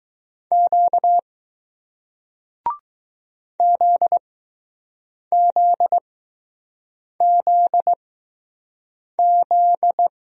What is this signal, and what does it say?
Morse code, heard as a keyed tone, about 20 wpm, spaced as for 6 wpm, 700 Hz. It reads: Q ZZZZ